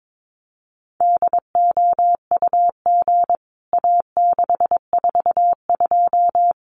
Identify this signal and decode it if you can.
Morse code, heard as a keyed tone, about 22 wpm, 700 Hz. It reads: DOUG A642
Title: CWT Exchange 22wpm